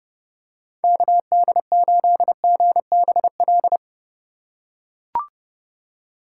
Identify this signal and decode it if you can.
Morse code, heard as a keyed tone, about 30 wpm, 700 Hz. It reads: KD8GBL